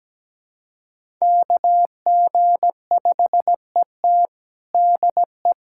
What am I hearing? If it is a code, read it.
Morse code, heard as a keyed tone, about 17 wpm, 700 Hz. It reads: KG5ET DE
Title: USA Callsign DE Callsign - 5 Characters - 17wpm